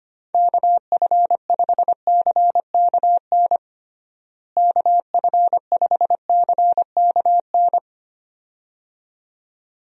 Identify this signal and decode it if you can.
Morse code, heard as a keyed tone, about 25 wpm, 700 Hz. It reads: KF5CKN KF5CKN
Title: USA Call Signs - 6 Characters - Repeated 2x - 25wpm